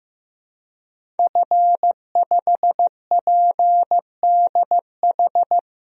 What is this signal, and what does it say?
Morse code, heard as a keyed tone, about 15 wpm, 700 Hz. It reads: F5PDH
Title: DXCC-100 Call Signs - 5 Characters - 15wpm